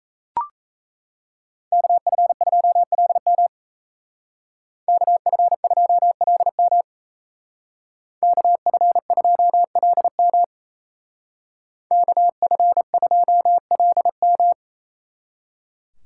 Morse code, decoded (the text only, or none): KF2LM KF2LM KF2LM KF2LM